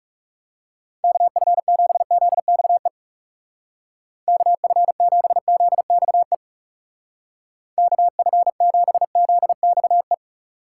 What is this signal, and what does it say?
Morse code, heard as a keyed tone, about 40 wpm, 700 Hz. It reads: KF7ZXE KF7ZXE KF7ZXE